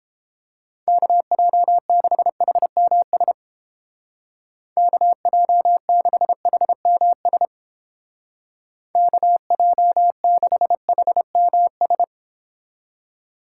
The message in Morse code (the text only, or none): KJ6HMS KJ6HMS KJ6HMS